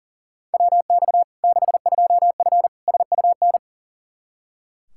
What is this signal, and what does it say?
Morse code, heard as a keyed tone, about 40 wpm, 700 Hz. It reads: WX 62F SUN
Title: Practice QSO - Exchange 2 - Info - 40wpm